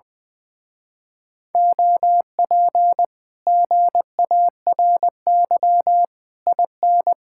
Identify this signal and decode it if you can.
Morse code, heard as a keyed tone, about 20 wpm, 700 Hz. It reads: OP GARY IN
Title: Speed-Racing Common QSO Phrases - 20wpm